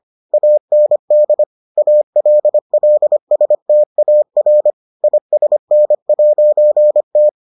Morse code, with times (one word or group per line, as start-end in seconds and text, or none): 0.3-1.4 AND
1.8-4.7 ALLSTAR
5.0-7.3 ISN'T